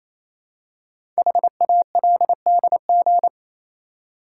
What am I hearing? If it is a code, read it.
Morse code, heard as a keyed tone, about 28 wpm, 700 Hz. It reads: HALDG
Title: Code Groups - 5 Letters - 28wpm